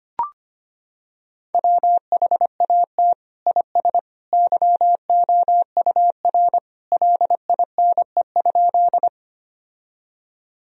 Morse code, heard as a keyed tone, about 25 wpm, 700 Hz. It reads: WHAT IS YOUR LINE?
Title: Sentences from Top 1000 Words - Review Part 1 - 25wpm